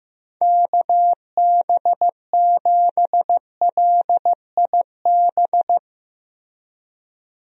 Morse code, heard as a keyed tone, about 15 wpm, 700 Hz. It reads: KB7LIB
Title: USA Call Signs - 6 Characters - Repeated 2x - 15wpm